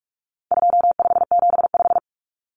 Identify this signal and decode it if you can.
Morse code, heard as a keyed tone, about 44 wpm, 700 Hz. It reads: 2575
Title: Numbers - 4-Digit 45wpm